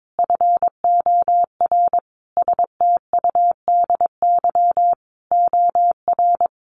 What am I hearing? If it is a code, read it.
Morse code, heard as a keyed tone, about 22 wpm, 700 Hz. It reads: FOR STUDY OR
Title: Sentences from Top 1200 Words 22wpm